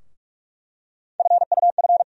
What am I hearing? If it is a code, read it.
Morse code, heard as a keyed tone, about 47 wpm, 700 Hz. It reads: FUF
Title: USA Call Sign Postfixes - 3 Letters 45wpm